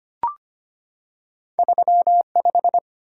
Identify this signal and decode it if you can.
Morse code, heard as a keyed tone, about 25 wpm, 700 Hz. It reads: 35
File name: Two Letter-Number Combinations - 25wpm